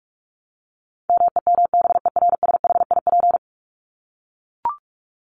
Morse code, heard as a keyed tone, about 44 wpm, 700 Hz. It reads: MEMBERSHIP
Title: Top 1251-1500 Words 45wpm